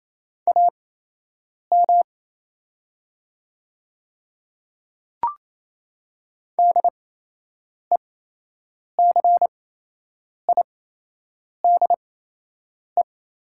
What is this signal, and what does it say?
Morse code, heard as a keyed tone, about 28 wpm, spaced as for 8 wpm, 700 Hz. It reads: AM DECIDE
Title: 8x Character Spacing - Top 500 Words - 28wpm